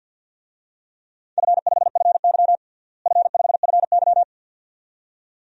Morse code, heard as a keyed tone, about 52 wpm, 700 Hz. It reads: F5FY F5FY
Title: DXCC-100 Call Signs - 4 Characters - Repeated 2x - 50wpm